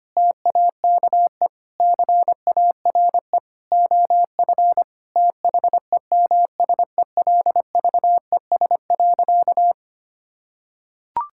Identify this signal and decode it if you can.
Morse code, heard as a keyed tone, about 25 wpm, 700 Hz. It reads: TAKE CARE OF THEMSELVES.
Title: Sentences from Top 1000 Words - Review Part 5 - 25wpm